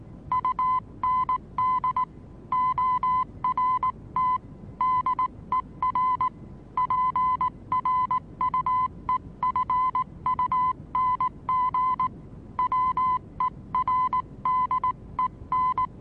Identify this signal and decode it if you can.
Morse code, heard as a keyed tone, about 19 wpm, 1010 Hz. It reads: UND ORT DER PRUEFUNG WERDEN
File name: repos / Morsekurs / DJ4UF Morsen26